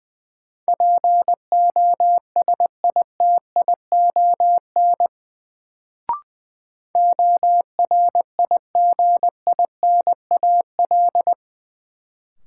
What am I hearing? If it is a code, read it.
Morse code, heard as a keyed tone, about 21 wpm, 700 Hz. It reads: POSITION ORIGINAL
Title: Sentences from Top 900 Words - New Word Drill - 20wpm